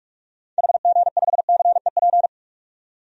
Morse code, heard as a keyed tone, about 47 wpm, 700 Hz. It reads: HG5CEP
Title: DXCC-100 Call Signs - 6 Characters - 45wpm